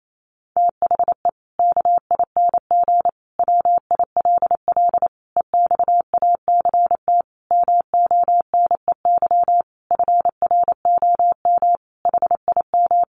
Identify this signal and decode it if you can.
Morse code, heard as a keyed tone, about 28 wpm, 700 Hz. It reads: THE KING WILL EXACT MONEY FROM HIM
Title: Sentences from Top 1300 Words 28wpm